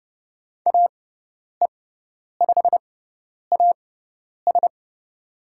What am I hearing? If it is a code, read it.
Morse code, heard as a keyed tone, about 30 wpm, spaced as for 10 wpm, 700 Hz. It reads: AE5AS
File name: Lesson 16g - Letter c - Call Signs 30wpm